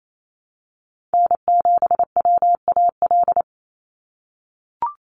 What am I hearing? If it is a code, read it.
Morse code, heard as a keyed tone, about 28 wpm, 700 Hz. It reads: N7WAL
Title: USA Call Signs 28wpm